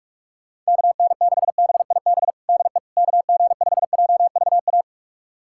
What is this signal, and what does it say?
Morse code, heard as a keyed tone, about 42 wpm, 700 Hz. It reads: KN6BIB DE KG5JVA
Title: USA Callsign DE Callsign - 6 Characters - 45wpm